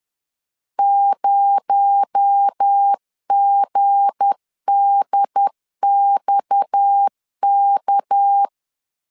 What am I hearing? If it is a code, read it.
Morse code, heard as a keyed tone, about 10 wpm, 800 Hz. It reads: 0GDXK
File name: LEZIONI / 8 YJ45 / 8 YJ 10wpm 2651